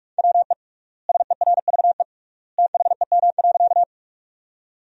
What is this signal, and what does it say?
Morse code, heard as a keyed tone, about 46 wpm, 700 Hz. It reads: WE SERVE THEM.
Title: 3x Word-Spacing - Sentences from Top 400 Words - 45wpm